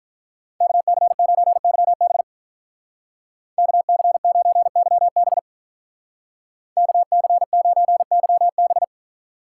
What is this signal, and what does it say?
Morse code, heard as a keyed tone, about 43 wpm, 700 Hz. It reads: KC9YB KC9YB KC9YB
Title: Speed Racing - USA 2x2 Call Signs 35wpm